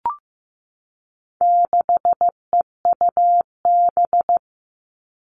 Morse code, heard as a keyed tone, about 15 wpm, 700 Hz. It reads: 6EUB
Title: All Cut Numbers - 4-Digit 15wpm